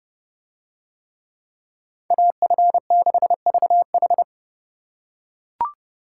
Morse code, heard as a keyed tone, about 29 wpm, 700 Hz. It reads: AF6VH